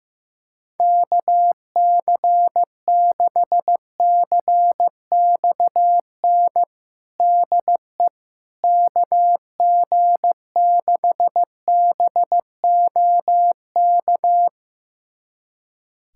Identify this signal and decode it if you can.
Morse code, heard as a keyed tone, about 15 wpm, 700 Hz. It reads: KC6CXN DE KG6BOK
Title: USA Callsign DE Callsign - 6 Characters - 15wpm